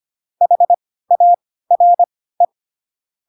Morse code, heard as a keyed tone, about 25 wpm, spaced as for 16 wpm, 700 Hz. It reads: HARE
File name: Pinocchio / Pinocchio ch4 10 25 ik1hge